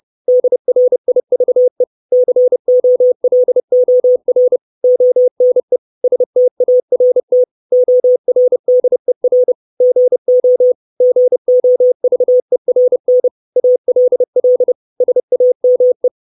Morse code, heard as a keyed tone, about 30 wpm, 500 Hz. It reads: DRIVE COLOR ONE START ORDER GO GOVERN ALL SAME
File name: 632 words / 632 Words 30WPM